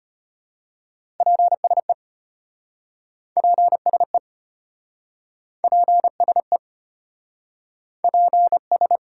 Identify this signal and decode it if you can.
Morse code, heard as a keyed tone, about 33 wpm, 700 Hz. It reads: PSE PSE PSE PS